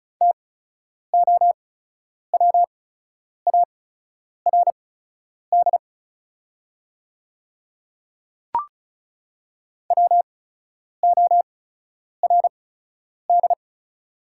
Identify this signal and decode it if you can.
Morse code, heard as a keyed tone, about 36 wpm, spaced as for 10 wpm, 700 Hz. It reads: TOWARD WORD
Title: 8x Character Spacing - Top 500 Words - 35wpm